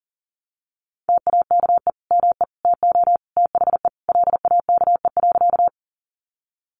Morse code, heard as a keyed tone, about 40 wpm, 700 Hz. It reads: TAKE ME TO THE LAKE.
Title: Sentences from Top 1100 Words 40wpm